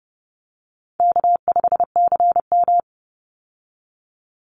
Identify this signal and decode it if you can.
Morse code, heard as a keyed tone, about 30 wpm, 700 Hz. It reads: K5CM
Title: Call Signs Repeated 3x - 30wpm